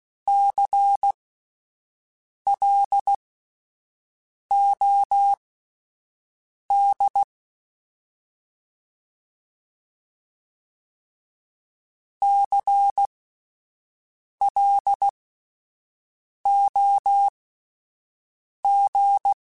Morse code, heard as a keyed tone, about 16 wpm, spaced as for 5 wpm, 800 Hz. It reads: CLOD CLOG
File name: Spaced Alphas / Spaced Alphas mp3s / Cs sp @10